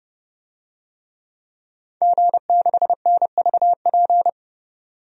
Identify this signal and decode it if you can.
Morse code, heard as a keyed tone, about 30 wpm, 700 Hz. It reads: G6NVP
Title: DXCC-100 Call Signs - 5 Characters - 30wpm